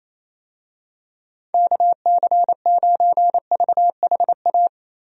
Morse code, heard as a keyed tone, about 27 wpm, 700 Hz. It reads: KC9VHA
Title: USA Call Signs - 6 Characters - Repeated 3x - 28wpm